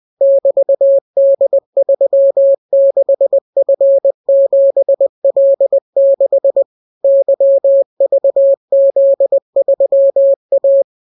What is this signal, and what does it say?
Morse code, heard as a keyed tone, about 20 wpm, 550 Hz. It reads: <BT>D36F7L6 YVZ3A